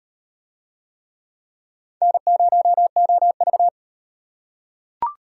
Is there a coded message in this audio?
Yes: Morse code from 2.0 to 3.7 s, heard as a keyed tone, about 33 wpm, 700 Hz. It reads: N0OV